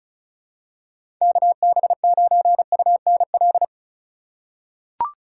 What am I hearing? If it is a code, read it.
Morse code, heard as a keyed tone, about 36 wpm, 700 Hz. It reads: KB9UNL
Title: USA Call Signs - 6 Characters - 3x Character Spacing - 35wpm